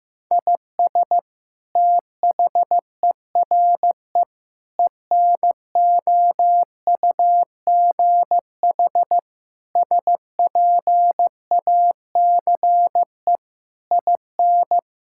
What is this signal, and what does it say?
Morse code, heard as a keyed tone, about 16 wpm, 700 Hz. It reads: IS THERE ENOUGH SPACE IN
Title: Sentences from Top 1000 Words - Review Part 5 - 15wpm